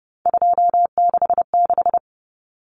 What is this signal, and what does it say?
Morse code, heard as a keyed tone, about 30 wpm, 700 Hz. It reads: 266
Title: Common Cut Numbers - 3-Digit 30wpm